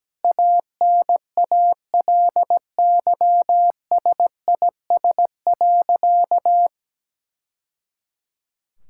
Morse code, heard as a keyed tone, about 17 wpm, 700 Hz. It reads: ANALYSIS.
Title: Sentences from Top 1000 Words - Review Part 5 - 17wpm